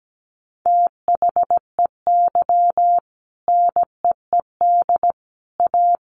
Morse code, heard as a keyed tone, about 17 wpm, 700 Hz. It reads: THEY NEED A
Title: Sentences from Top 500 Words 17wpm